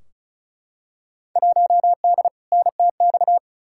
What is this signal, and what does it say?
Morse code, heard as a keyed tone, about 36 wpm, 700 Hz. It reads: 1D NTX
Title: Field Day Exchanges - 35wpm